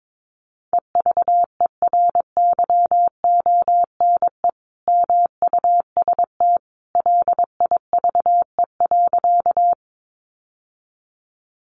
Morse code, heard as a keyed tone, about 22 wpm, 700 Hz. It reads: EVERYONE MUST LIVE.
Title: Sentences from Top 800 Words 22wpm